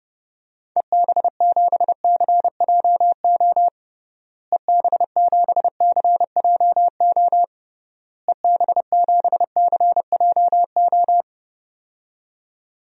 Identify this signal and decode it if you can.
Morse code, heard as a keyed tone, about 32 wpm, 700 Hz. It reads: EB7CJO EB7CJO EB7CJO
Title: DXCC-100 Call Signs - 6 Characters - Repeated 3x - 30wpm